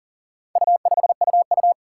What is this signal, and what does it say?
Morse code, heard as a keyed tone, about 40 wpm, 700 Hz. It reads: U5UU
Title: Four Challenging Letter-Number Combinations 40wpm